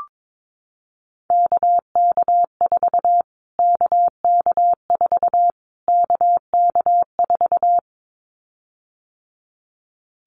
Morse code, heard as a keyed tone, about 22 wpm, 700 Hz. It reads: KK4 KK4 KK4
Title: Callsign Prefixes for Top 25 DXCC - Repeated 3x - 22wpm